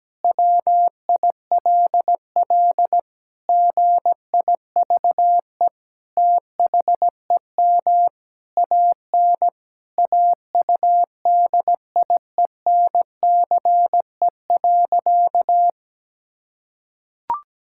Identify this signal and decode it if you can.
Morse code, heard as a keyed tone, about 17 wpm, 700 Hz. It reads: WILL GIVE THEM AN AUDIENCE.
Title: Sentences from Top 1000 Words - Review Part 5 - 17wpm